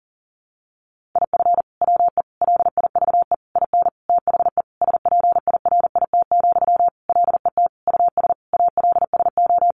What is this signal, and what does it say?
Morse code, heard as a keyed tone, about 37 wpm, 700 Hz. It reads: IF WE LIVE IN THE SPIRIT, LET US ALSO
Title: Sentences from Top 1400 Words 40wpm